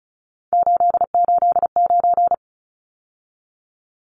Morse code, heard as a keyed tone, about 34 wpm, 700 Hz. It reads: 889
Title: Numbers - 3-Digit 35wpm